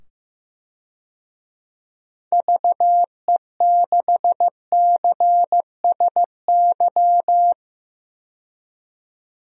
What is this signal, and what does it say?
Morse code, heard as a keyed tone, about 15 wpm, 700 Hz. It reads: VE6CSY
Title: DXCC-100 Call Signs - 6 Characters - 15wpm